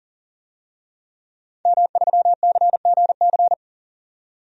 Morse code, heard as a keyed tone, about 41 wpm, 700 Hz. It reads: M3CGC
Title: DXCC-100 Call Signs - 5 Characters - 40wpm